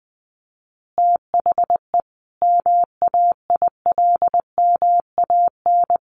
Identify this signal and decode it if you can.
Morse code, heard as a keyed tone, about 19 wpm, 700 Hz. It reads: THE MAILMAN
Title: Easy Phrases 20wpm